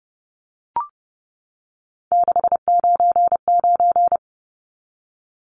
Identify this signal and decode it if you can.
Morse code, heard as a keyed tone, about 30 wpm, 700 Hz. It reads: 699
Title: Numbers - 3-Digit 30wpm